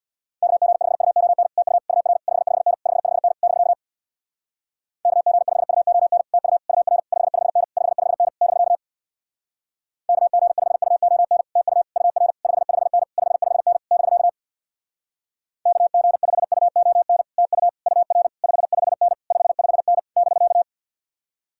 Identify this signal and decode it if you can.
Morse code, heard as a keyed tone, about 64 wpm, 700 Hz. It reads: KG5UON TU UR 55N 55N <BK> KG5UON TU UR 55N 55N <BK> KG5UON TU UR 55N 55N <BK> KG5UON TU UR 55N 55N <BK>